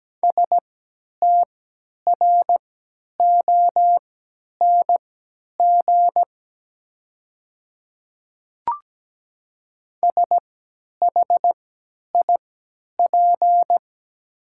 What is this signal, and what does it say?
Morse code, heard as a keyed tone, about 17 wpm, spaced as for 10 wpm, 700 Hz. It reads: STRONG SHIP
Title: 3x Character Spacing - Top 500 Words - 17wpm